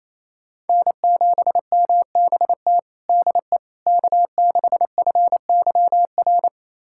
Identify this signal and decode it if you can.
Morse code, heard as a keyed tone, about 28 wpm, 700 Hz. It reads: N7MBT DE K6FYR
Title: USA Callsign DE Callsign - 5 Characters - 28wpm